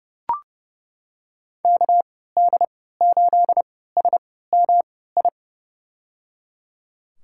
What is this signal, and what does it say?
Morse code, heard as a keyed tone, about 31 wpm, spaced as for 17 wpm, 690 Hz. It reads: KD8SMI
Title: USA Call Signs - 3x Character Spacing - 30wpm